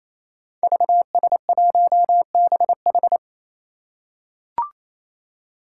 Morse code, heard as a keyed tone, about 28 wpm, 700 Hz. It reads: VS1BH
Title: Five Challenging Letter-Number Combinations 28wpm